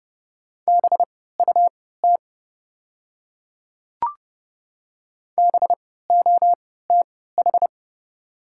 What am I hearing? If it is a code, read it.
Morse code, heard as a keyed tone, about 32 wpm, spaced as for 17 wpm, 700 Hz. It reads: BUT BOTH
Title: 3x Character Spacing - Top 300 Words - 30wpm